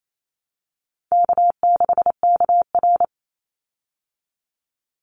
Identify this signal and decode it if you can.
Morse code, heard as a keyed tone, about 28 wpm, 700 Hz. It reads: K6KR
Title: Easy Call Signs Repeated 2x - 28wpm